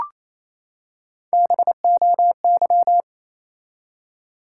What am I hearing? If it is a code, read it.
Morse code, heard as a keyed tone, about 28 wpm, 700 Hz. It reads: BOY
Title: Three Letter Words - 28wpm